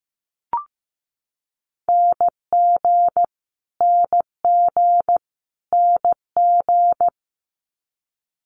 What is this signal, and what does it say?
Morse code, heard as a keyed tone, about 15 wpm, 700 Hz. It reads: NG NG NG
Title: Multi-letter Phonemes Repeated 3x 15wpm